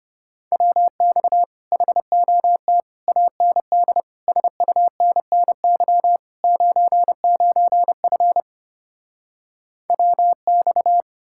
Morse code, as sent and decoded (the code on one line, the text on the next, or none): .-- -..- / .... --- - / .- -. -.. / ... ..- -. -. -.-- / ----. ----. ..-. / .-- -..-
WX HOT AND SUNNY 99F WX